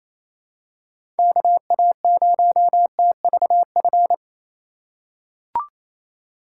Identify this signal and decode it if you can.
Morse code, heard as a keyed tone, about 28 wpm, 700 Hz. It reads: KA0TVF
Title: USA Call Signs - 3x Character Spacing - 28wpm